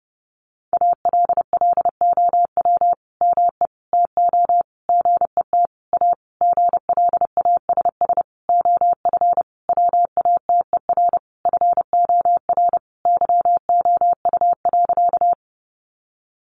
Morse code, heard as a keyed tone, about 30 wpm, 700 Hz. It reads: ALLOW ME TO GET A GLASS OF WATER FOR YOU.
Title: Sentences from Top 1300 Words 30wpm